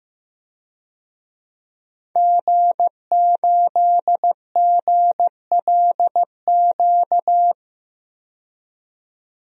Morse code, heard as a keyed tone, about 15 wpm, 700 Hz. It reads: G8GLQ
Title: DXCC-100 Call Signs - 5 Characters - 15wpm